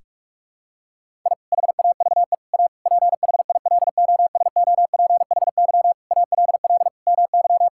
Morse code, heard as a keyed tone, about 42 wpm, 700 Hz. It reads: I HAVE A PHILOSOPHY ALL MY
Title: Sentences from Top 1000 Words - Review Part 6 - 45wpm